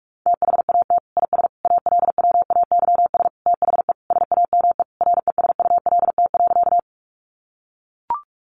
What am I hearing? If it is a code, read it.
Morse code, heard as a keyed tone, about 44 wpm, 700 Hz. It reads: THAT IS ALWAYS THE SAME RESULT.
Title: Sentences from Top 700 Words 45wpm